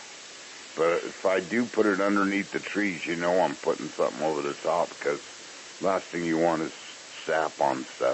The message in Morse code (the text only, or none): none